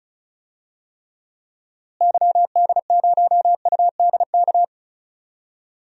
Morse code, heard as a keyed tone, about 36 wpm, 700 Hz. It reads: YD0UDK
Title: DXCC-100 Call Signs - 6 Characters - Repeated 3x - 35wpm